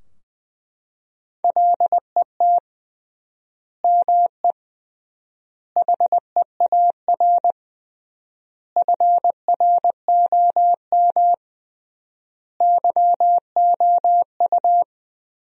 Morse code, heard as a keyed tone, about 20 wpm, 700 Hz. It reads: LET ME HEAR FROM YOU
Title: 3x Word-Spacing - Sentences from Top 400 Words - 20wpm